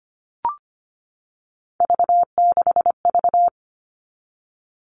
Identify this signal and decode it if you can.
Morse code, heard as a keyed tone, about 25 wpm, 700 Hz. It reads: V6V
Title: All Cut Numbers - 3-Digit 25wpm